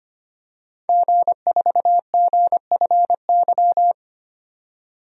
Morse code, heard as a keyed tone, about 25 wpm, 700 Hz. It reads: G4GFY
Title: DXCC-100 Call Signs - 5 Characters - 25wpm